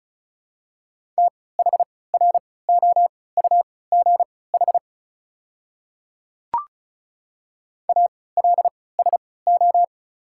Morse code, heard as a keyed tone, about 36 wpm, spaced as for 20 wpm, 690 Hz. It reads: THROUGH ALSO
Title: 3x Character Spacing - Top 500 Words - 35wpm